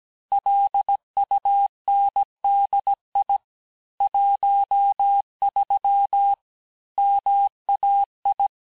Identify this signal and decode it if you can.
Morse code, heard as a keyed tone, about 17 wpm, 800 Hz. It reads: LUNDI 13 MAI
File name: cours-du-lundi files / corrige-cours-1581-texte-chiffre-vitesse-16-mpm